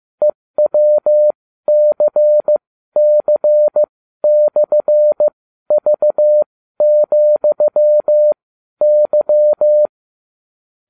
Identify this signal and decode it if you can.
Morse code, heard as a keyed tone, about 15 wpm, spaced as for 12 wpm, 600 Hz. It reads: EWCC/V,Y